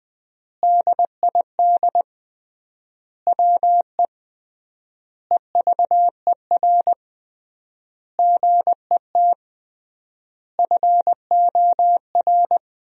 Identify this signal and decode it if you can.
Morse code, heard as a keyed tone, about 20 wpm, 700 Hz. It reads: DID WE EVER GET FOR
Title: 3x Word-Spacing - Sentences from Top 400 Words - 20wpm